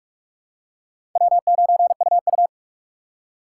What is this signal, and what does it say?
Morse code, heard as a keyed tone, about 44 wpm, 700 Hz. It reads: W9UU
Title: USA Call Signs - 4 Characters - 6x Character Spacing - 45wpm